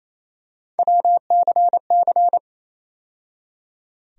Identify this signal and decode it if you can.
Morse code, heard as a keyed tone, about 28 wpm, 700 Hz. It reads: WCC
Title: USA Call Sign Postfixes - 3 Letters 28wpm